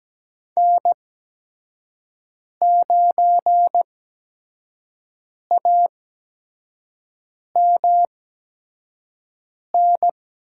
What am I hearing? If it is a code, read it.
Morse code, heard as a keyed tone, about 17 wpm, spaced as for 5 wpm, 700 Hz. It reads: N9AMN
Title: USA Call Signs - 5 Characters - 8x Character Spacing - 17wpm